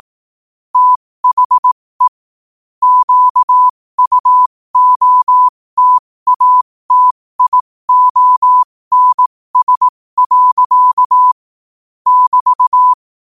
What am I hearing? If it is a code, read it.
Morse code, heard as a keyed tone, about 18 wpm, spaced as for 15 wpm, 1000 Hz. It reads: THE QUOTATIONS. <BT>
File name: qotdfiles / 15wpm